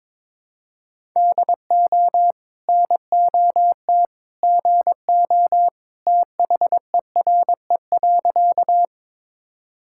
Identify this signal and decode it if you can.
Morse code, heard as a keyed tone, about 22 wpm, 700 Hz. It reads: DO NOT GO THERE.
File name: practice / 22wpm / Sentences with 2-Word Phrases - 22wpm